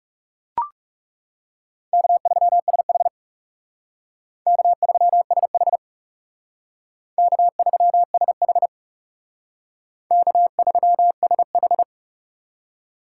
Morse code, heard as a keyed tone, about 37 wpm, 700 Hz. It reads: K3SH K3SH K3SH K3SH
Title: Speed Racing - USA 1x2 Call Signs 30wpm